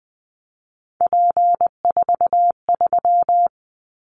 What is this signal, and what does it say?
Morse code, heard as a keyed tone, about 20 wpm, 700 Hz. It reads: P43
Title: Callsign Prefixes for Top 100 DXCC - 20wpm